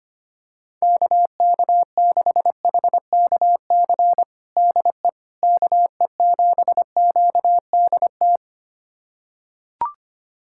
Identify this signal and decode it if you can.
Morse code, heard as a keyed tone, about 25 wpm, 700 Hz. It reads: KK6HKC DE KE7QDT
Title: USA Callsign DE Callsign - 6 Characters - 25wpm